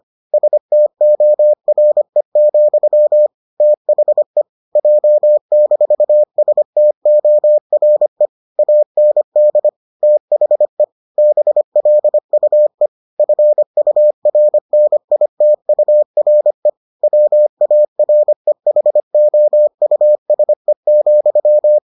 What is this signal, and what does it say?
Morse code, heard as a keyed tone, about 25 wpm, 600 Hz. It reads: STORE, THE J-STORE AND THE BLUE FURNITURE WAREHOUSE,